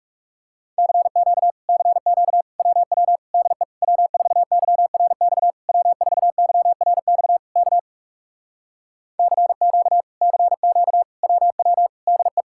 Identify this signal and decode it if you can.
Morse code, heard as a keyed tone, about 40 wpm, 700 Hz. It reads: CQ CQ WW DE W4YRX W4YRX K CQ CQ WW DE